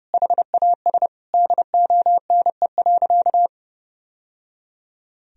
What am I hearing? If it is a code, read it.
Morse code, heard as a keyed tone, about 28 wpm, 700 Hz. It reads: HAS DONE.
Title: Sentences from Top 1000 Words - Review Part 1 - 30wpm